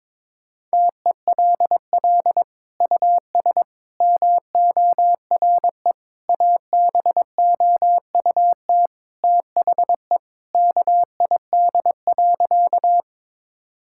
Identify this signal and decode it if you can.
Morse code, heard as a keyed tone, about 22 wpm, 700 Hz. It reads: TELL US MORE ABOUT THE KID.
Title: Sentences from Top 1600 Words 22wpm